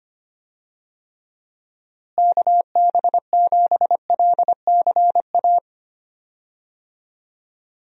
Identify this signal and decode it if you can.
Morse code, heard as a keyed tone, about 25 wpm, 700 Hz. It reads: KB7LCA
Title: USA Call Signs - 6 Characters - Repeated 2x - 25wpm